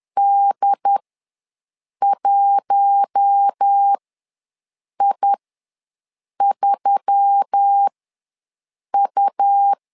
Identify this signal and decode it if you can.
Morse code, heard as a keyed tone, about 10 wpm, spaced as for 6 wpm, 800 Hz. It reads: D1I3U
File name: LEZIONI / 4 Q1S3 ws 10wpm 2437